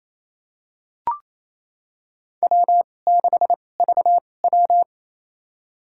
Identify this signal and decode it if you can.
Morse code, heard as a keyed tone, about 28 wpm, spaced as for 20 wpm, 700 Hz. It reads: W6VW
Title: USA Call Signs - 2x Character Spacing - 28wpm